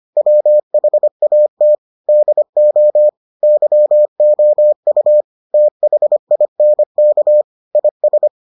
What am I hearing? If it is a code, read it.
Morse code, heard as a keyed tone, about 25 wpm, 600 Hz. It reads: WHAT DO YOU THINK IS